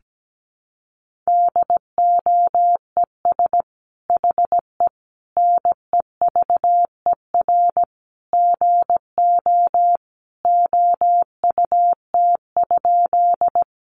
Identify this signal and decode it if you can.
Morse code, heard as a keyed tone, about 17 wpm, 700 Hz. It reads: DOES HE NEVER GO OUT?
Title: Sentences from Top 300 Words 17wpm